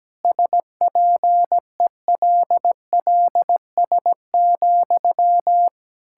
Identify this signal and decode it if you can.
Morse code, heard as a keyed tone, about 17 wpm, 700 Hz. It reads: SPELLS,
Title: Pangram Challenge v2 - 17wpm